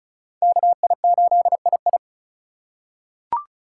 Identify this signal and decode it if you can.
Morse code, heard as a keyed tone, about 36 wpm, 690 Hz. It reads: KI8II
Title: USA Call Signs - 5 Characters - 2x Character Spacing - 35wpm